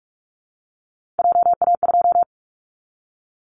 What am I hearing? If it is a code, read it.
Morse code, heard as a keyed tone, about 46 wpm, 700 Hz. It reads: JA2
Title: Callsign Prefixes for Top 50 DXCC - 45wpm